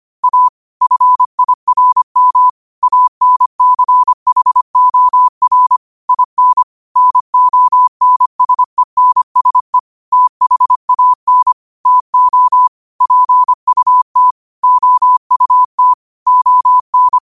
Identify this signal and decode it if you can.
Morse code, heard as a keyed tone, about 25 wpm, 1000 Hz. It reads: A FIRM ANCHOR IN NONSENSE THAN TO PUT OUT ON
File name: 25wpm